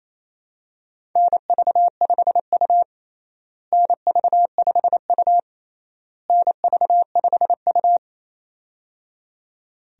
Morse code, heard as a keyed tone, about 28 wpm, 700 Hz. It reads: NV5U NV5U NV5U
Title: USA Call Signs - 4 Characters - Repeated 3x - 28wpm